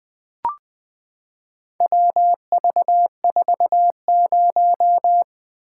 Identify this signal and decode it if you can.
Morse code, heard as a keyed tone, about 20 wpm, 700 Hz. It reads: WV40
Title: Four Letter-Number Combinations - 20wpm